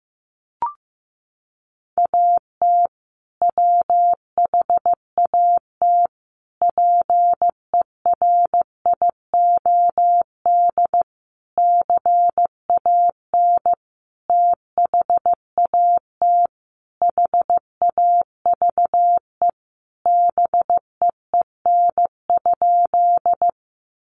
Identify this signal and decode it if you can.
Morse code, heard as a keyed tone, about 15 wpm, 700 Hz. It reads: AT WHAT PERIOD CAN THAT HAVE BEEN?
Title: Sentences from Top 800 Words 15wpm